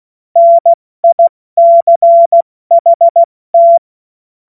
Morse code, heard as a keyed tone, about 16 wpm, 680 Hz. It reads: NICHT
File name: Morsekurs / Witze / Witz 16 April 2024 70 F80